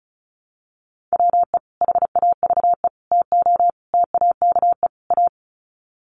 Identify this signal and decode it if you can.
Morse code, heard as a keyed tone, about 34 wpm, 700 Hz. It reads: WE HAVE TO TAKE A
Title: Sets of 5 Words Encore 35wpm